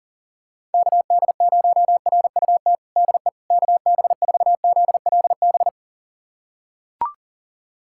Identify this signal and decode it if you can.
Morse code, heard as a keyed tone, about 40 wpm, 700 Hz. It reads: KD0RUT DE KB4ZLB
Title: USA Callsign DE Callsign - 6 Characters - 40wpm